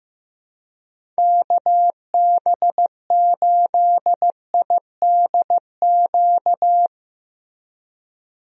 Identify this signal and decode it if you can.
Morse code, heard as a keyed tone, about 15 wpm, 700 Hz. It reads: KB8IDQ
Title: USA Call Signs - 6 Characters - Repeated 3x - 15wpm